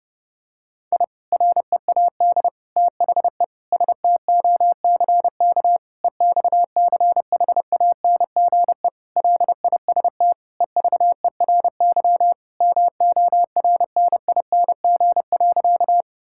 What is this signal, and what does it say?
Morse code, heard as a keyed tone, about 31 wpm, 700 Hz. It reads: I READ THE STOCK EXCHANGE LIST EVERY MORNING.
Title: Sentences from Top 1000 Words - Review Part 6 - 30wpm